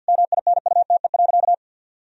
Morse code, heard as a keyed tone, about 52 wpm, 700 Hz. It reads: MINUTE.
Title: Sentences from Top 1000 Words - Review Part 2 - 50wpm